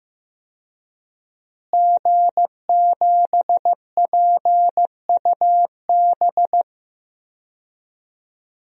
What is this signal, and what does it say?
Morse code, heard as a keyed tone, about 15 wpm, 700 Hz. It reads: G7PUB
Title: DXCC-100 Call Signs - 5 Characters - Repeated 3x - 15wpm